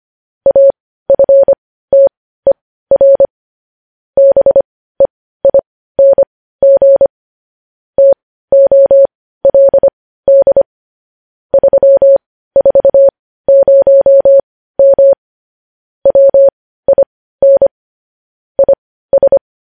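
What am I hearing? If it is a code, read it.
Morse code, heard as a keyed tone, about 25 wpm, spaced as for 15 wpm, 550 Hz. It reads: AFTER BEING TOLD 340M WIN IS
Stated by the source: Morse code news headlines sent at 15 words per minute.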